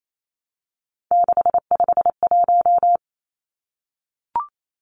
Morse code, heard as a keyed tone, about 28 wpm, 700 Hz. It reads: 651
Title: Numbers - 3-Digit 28wpm